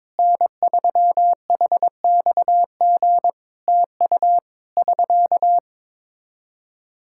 Morse code, heard as a keyed tone, about 22 wpm, 700 Hz. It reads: N3HXG TU <SK>
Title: Practice QSO - Exchange 4 - Ending - 22wpm